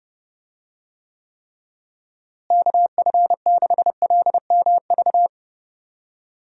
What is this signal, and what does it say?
Morse code, heard as a keyed tone, about 32 wpm, 700 Hz. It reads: KF6LMV
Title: USA Call Signs - 6 Characters - Repeated 3x - 30wpm